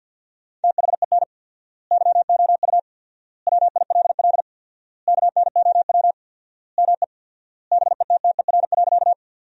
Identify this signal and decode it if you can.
Morse code, heard as a keyed tone, about 50 wpm, 700 Hz. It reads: THEN YOU WILL KNOW ME BETTER.
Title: 4x Word-Spacing - Sentences from Top 400 Words - 50wpm